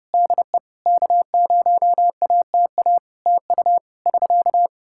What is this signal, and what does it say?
Morse code, heard as a keyed tone, about 30 wpm, 700 Hz. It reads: DE K0ATA TU <SK>
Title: Practice QSO - Exchange 4 - Ending - 30wpm